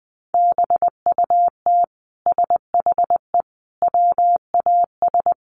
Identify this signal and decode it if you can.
Morse code, heard as a keyed tone, about 20 wpm, 700 Hz. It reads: BUT SHE WAS
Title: Sentences from Top 1000 Words 20wpm